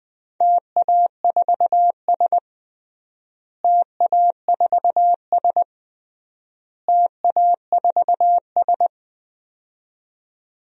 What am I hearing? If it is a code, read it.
Morse code, heard as a keyed tone, about 21 wpm, 700 Hz. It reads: TA4S TA4S TA4S
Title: DXCC-100 Call Signs - 4 Characters - Repeated 3x - 20wpm